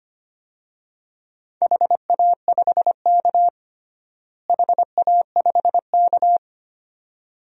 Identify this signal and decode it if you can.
Morse code, heard as a keyed tone, about 25 wpm, 700 Hz. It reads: HA5K HA5K
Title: DXCC-100 Call Signs - 4 Characters - Repeated 2x - 25wpm